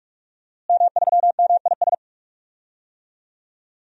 Morse code, heard as a keyed tone, about 42 wpm, 700 Hz. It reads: M3MIS
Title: DXCC-100 Call Signs - 5 Characters - Repeated 2x - 45wpm